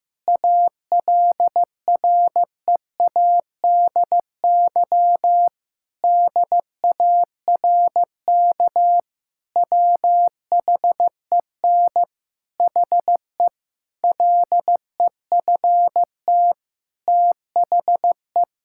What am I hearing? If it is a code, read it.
Morse code, heard as a keyed tone, about 15 wpm, 700 Hz. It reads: ALREADY DARK WHEN HE LEFT THE
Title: Sentences from Top 1000 Words - Review Part 5 - 15wpm